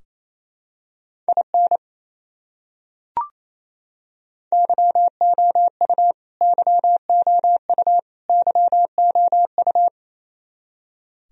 Morse code, heard as a keyed tone, about 28 wpm, 700 Hz. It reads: IN YOU YOU YOU
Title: Top 100 Words Repeated 28wpm